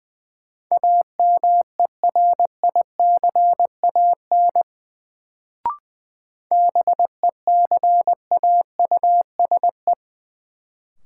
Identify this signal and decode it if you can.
Morse code, heard as a keyed tone, about 20 wpm, 700 Hz. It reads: AMERICAN BECAUSE
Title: Sets of 2 Words - New Word Drill - 20wpm